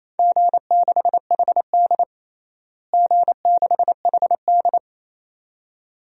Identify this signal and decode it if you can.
Morse code, heard as a keyed tone, about 28 wpm, 700 Hz. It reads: G6HD G6HD
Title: DXCC-100 Call Signs - 4 Characters - Repeated 2x - 28wpm